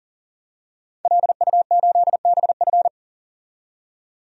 Morse code, heard as a keyed tone, about 39 wpm, 700 Hz. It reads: LU8BF
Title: DXCC-100 Call Signs - 5 Characters - 40wpm